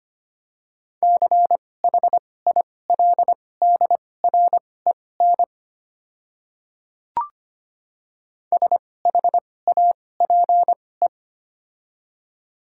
Morse code, heard as a keyed tone, about 25 wpm, spaced as for 18 wpm, 700 Hz. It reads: CHILDREN SHAPE
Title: 2x Character Spacing - Top 500 Words - 25wpm